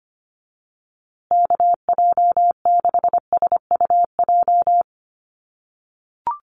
Morse code, heard as a keyed tone, about 25 wpm, 700 Hz. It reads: KJ6SUJ
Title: USA Call Signs Repeated 2x 25wpm